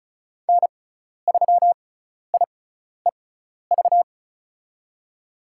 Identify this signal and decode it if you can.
Morse code, heard as a keyed tone, about 36 wpm, spaced as for 12 wpm, 700 Hz. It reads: N3IEV
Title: USA Call Signs - 5 Characters - 6x Character Spacing - 35wpm